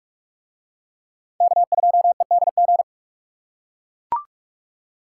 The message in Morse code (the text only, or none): K2EDG